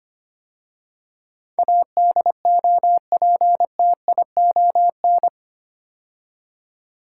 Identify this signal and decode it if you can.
Morse code, heard as a keyed tone, about 25 wpm, 700 Hz. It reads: ADOPTION
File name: 8 Letter Words - First 3 Letters Given - 25wpm